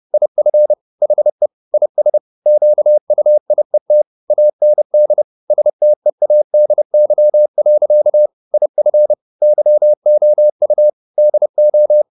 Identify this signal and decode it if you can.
Morse code, heard as a keyed tone, about 30 wpm, 600 Hz. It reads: IF HE IS QUIET AND STEADY. IF YOU DO